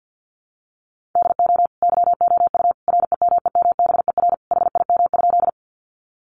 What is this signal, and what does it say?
Morse code, heard as a keyed tone, about 50 wpm, 700 Hz. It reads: DO YOU REMEMBER HIM?
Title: Sentences from Top 400 Words 50wpm